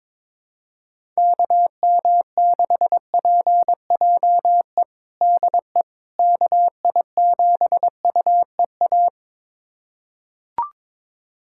Morse code, heard as a keyed tone, about 22 wpm, 700 Hz. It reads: KM6PJE DE KI7UEA
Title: USA Callsign DE Callsign 22wpm